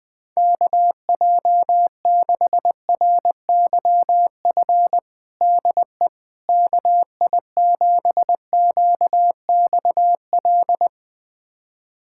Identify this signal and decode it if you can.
Morse code, heard as a keyed tone, about 20 wpm, 700 Hz. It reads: KJ6RYF DE KI7QXL
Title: USA Callsign DE Callsign - 6 Characters - 20wpm